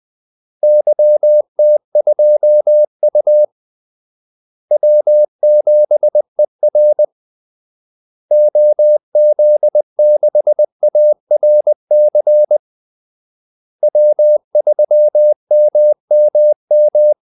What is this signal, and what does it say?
Morse code, heard as a keyed tone, about 20 wpm, 600 Hz. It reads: YT2U W7ER OZ6ARC W3MMM
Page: Kaldesignaler 20 wpm